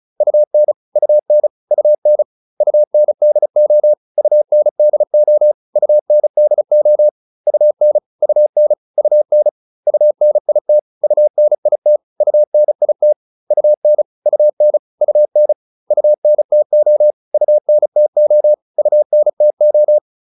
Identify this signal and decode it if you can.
Morse code, heard as a keyed tone, about 34 wpm, 600 Hz. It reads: UN UN UN UNDO UNDO UNDO UN UN UN UNIT UNIT UNIT UN UN UN UNTO UNTO UNTO